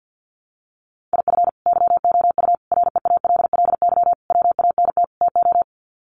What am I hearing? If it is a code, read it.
Morse code, heard as a keyed tone, about 50 wpm, 700 Hz. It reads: IF YOU REALLY WANT TO
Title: Sets of 5 Words Encore2 50wpm